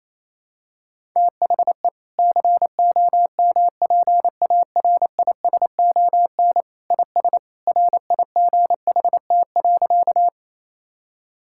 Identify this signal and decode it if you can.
Morse code, heard as a keyed tone, about 27 wpm, 700 Hz. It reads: THE COMPARISON IS RIGHT.
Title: Sentences from Top 1600 Words 28wpm